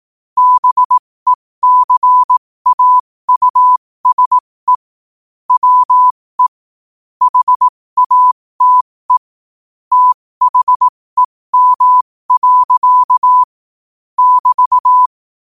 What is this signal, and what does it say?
Morse code, heard as a keyed tone, about 18 wpm, spaced as for 15 wpm, 1000 Hz. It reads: BECAUSE WE HATE THEM. <BT>
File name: qotdfiles / 15wpm